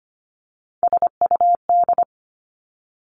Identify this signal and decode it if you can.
Morse code, heard as a keyed tone, about 25 wpm, 700 Hz. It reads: SUD
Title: Three Challenging Letter-Number Combinations 25wpm